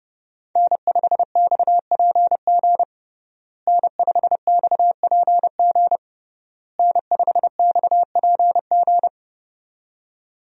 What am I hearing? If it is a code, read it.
Morse code, heard as a keyed tone, about 32 wpm, 700 Hz. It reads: N5XPG N5XPG N5XPG
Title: USA Call Signs - 5 Characters - Repeated 3x - 30wpm